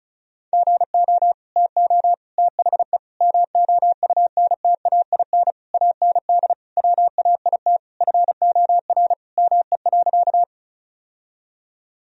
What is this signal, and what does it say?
Morse code, heard as a keyed tone, about 36 wpm, 700 Hz. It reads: GO TO THE MOUNTAIN AND WAIT FOR ME.
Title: Sentences from Top 1000 Words - Review Part 1 - 35wpm